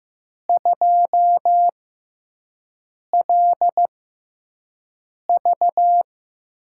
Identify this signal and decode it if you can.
Morse code, heard as a keyed tone, about 15 wpm, spaced as for 5 wpm, 700 Hz. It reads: 2LV
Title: USA Call Signs - 6 Characters - 6x Character Spacing - 15wpm